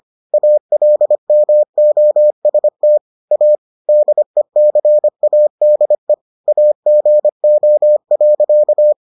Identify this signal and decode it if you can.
Morse code, heard as a keyed tone, about 25 wpm, 600 Hz. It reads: ALMOST A DECADE AGO.